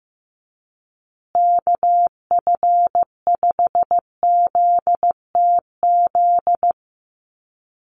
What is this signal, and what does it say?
Morse code, heard as a keyed tone, about 15 wpm, 700 Hz. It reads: KF5ZTZ
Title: USA Call Signs Repeated 3x 15wpm